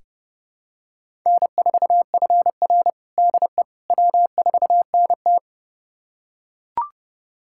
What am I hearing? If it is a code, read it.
Morse code, heard as a keyed tone, about 30 wpm, 700 Hz. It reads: N4FR DE W4NT
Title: USA Callsign DE Callsign - 4 Characters - 30wpm